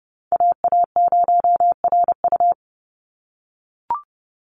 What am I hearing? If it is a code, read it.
Morse code, heard as a keyed tone, about 30 wpm, 700 Hz. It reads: AA0RU
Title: USA Call Signs 30wpm